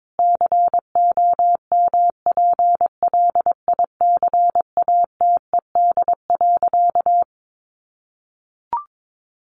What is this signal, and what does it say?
Morse code, heard as a keyed tone, about 22 wpm, 700 Hz. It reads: COMPLICATED.
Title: Sentences from Top 1200 Words 22wpm